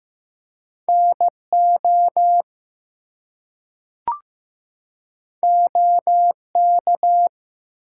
Morse code, heard as a keyed tone, about 15 wpm, 700 Hz. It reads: NO OK
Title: Two Letter Words - 15wpm